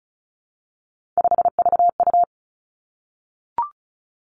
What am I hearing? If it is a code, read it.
Morse code, heard as a keyed tone, about 36 wpm, 700 Hz. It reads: 5VU
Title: Three Challenging Letter-Number Combinations 35wpm